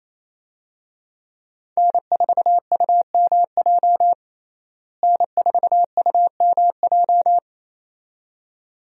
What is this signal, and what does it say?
Morse code, heard as a keyed tone, about 28 wpm, 700 Hz. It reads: N4UMJ N4UMJ
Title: USA Call Signs - 5 Characters - Repeated 2x - 28wpm